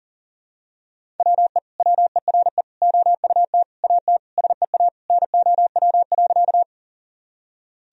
Morse code, heard as a keyed tone, about 40 wpm, 700 Hz. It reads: WE WERE OUT AT SEA NOW.
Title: Sentences from Top 1000 Words - Review Part 1 - 40wpm